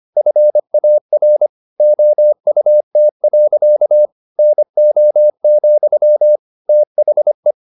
Morse code, heard as a keyed tone, about 25 wpm, 600 Hz. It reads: FAR OUT. NO, THE
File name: the old man and the sea part 1 25wpm